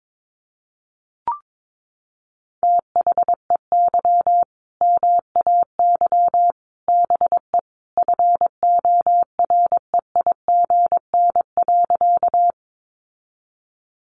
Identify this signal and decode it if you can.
Morse code, heard as a keyed tone, about 22 wpm, 700 Hz. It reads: THEY MAY BE FOREIGN.
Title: Sentences from Top 1400 Words 22wpm